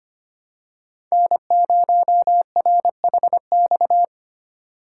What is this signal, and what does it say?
Morse code, heard as a keyed tone, about 25 wpm, 700 Hz. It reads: N0RHX
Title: USA Call Signs - 5 Characters - 4x Character Spacing - 25wpm